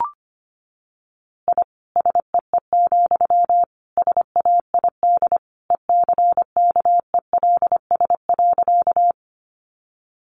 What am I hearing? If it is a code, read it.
Morse code, heard as a keyed tone, about 25 wpm, 700 Hz. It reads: I SEE, SAID ECKELS.
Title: A Sound Of Thunder 25wpm